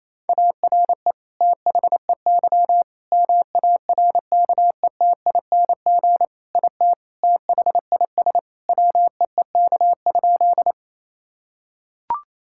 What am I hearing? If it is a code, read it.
Morse code, heard as a keyed tone, about 28 wpm, 700 Hz. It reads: ARE THEY MARKETING IT THIS WEEK?
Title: Sentences from Top 1000 Words - Review Part 5 - 28wpm